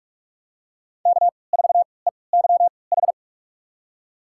Morse code, heard as a keyed tone, about 46 wpm, spaced as for 25 wpm, 700 Hz. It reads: K4EYH
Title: USA Call Signs - 5 Characters - 3x Character Spacing - 45wpm